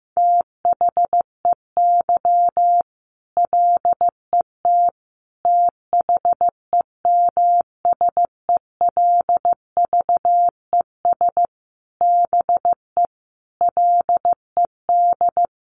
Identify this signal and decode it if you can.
Morse code, heard as a keyed tone, about 15 wpm, 700 Hz. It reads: THEY LET THEMSELVES BE LED
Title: A Sound Of Thunder 15wpm